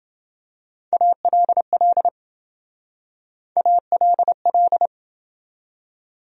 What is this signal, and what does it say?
Morse code, heard as a keyed tone, about 27 wpm, 700 Hz. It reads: ALL ALL